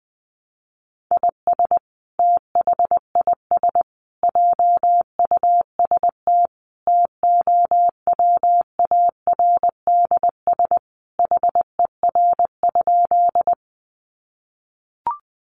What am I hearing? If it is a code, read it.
Morse code, heard as a keyed tone, about 20 wpm, 700 Hz. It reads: IS THIS JUST TOWARDS HER?
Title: Sentences from Top 1100 Words 20wpm